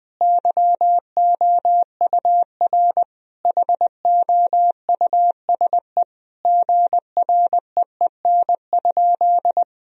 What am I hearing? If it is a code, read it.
Morse code, heard as a keyed tone, about 20 wpm, 700 Hz. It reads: YOUR HOUSE GREEN?
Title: Sentences from Top 1000 Words - Review Part 2 - 20wpm